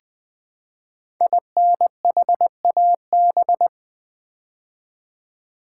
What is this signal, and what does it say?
Morse code, heard as a keyed tone, about 20 wpm, 700 Hz. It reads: INHAB
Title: Build A Word - 5 Letters At A Time - 10-Letter Words - 20wpm